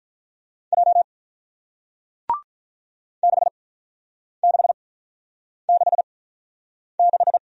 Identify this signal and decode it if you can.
Morse code, heard as a keyed tone, about 48 wpm, spaced as for 9 wpm, 700 Hz. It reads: J 6666